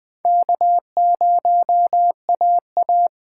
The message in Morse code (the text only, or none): K0AA